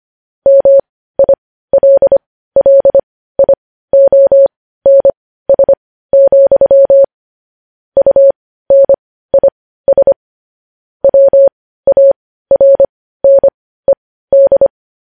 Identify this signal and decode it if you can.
Morse code, heard as a keyed tone, about 25 wpm, spaced as for 15 wpm, 550 Hz. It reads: MILLIONS, UNIS WARNED
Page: News Headlines in Morse Code at 15 WPM